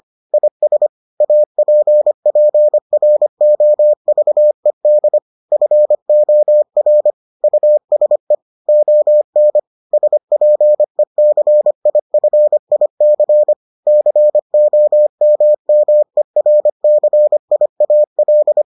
Morse code, heard as a keyed tone, about 25 wpm, 600 Hz. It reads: IS APPROVED FOR USE ON SPECIFIC COMMERCIAL